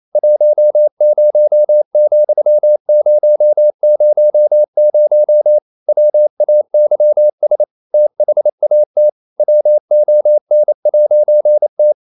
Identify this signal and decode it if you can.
Morse code, heard as a keyed tone, about 28 wpm, 600 Hz. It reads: 10,000 WAYS THAT WON'T
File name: CW / Practice / EnglishQuotations0228